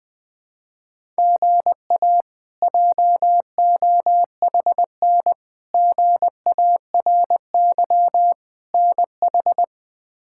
Morse code, heard as a keyed tone, about 20 wpm, 700 Hz. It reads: GA JOHN GARY NH
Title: K1USN SST QSO Practice - 20wpm